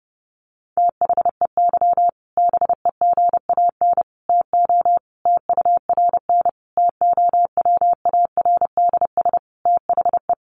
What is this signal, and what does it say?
Morse code, heard as a keyed tone, about 30 wpm, 700 Hz. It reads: THEY BEGAN TO TURN TOWARDS THE
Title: Sentences from Top 1100 Words 30wpm